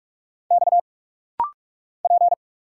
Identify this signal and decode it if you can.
Morse code, heard as a keyed tone, about 44 wpm, 700 Hz. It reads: X P
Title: Single Letter-Number Alt - Rapid-Fire 45wpm